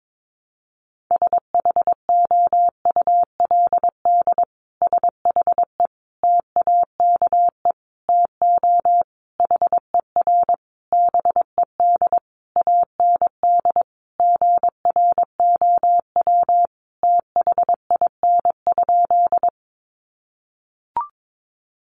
Morse code, heard as a keyed tone, about 22 wpm, 700 Hz. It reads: SHOULD SHE TAKE TO HER BED AND GROW THIN?
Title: Sentences from Top 1200 Words 22wpm